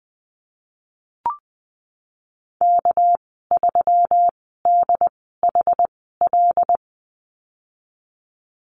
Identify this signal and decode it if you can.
Morse code, heard as a keyed tone, about 20 wpm, spaced as for 15 wpm, 700 Hz. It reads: K3DHL
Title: 207 Lesson 33g - Letter j - Call Signs 20wpm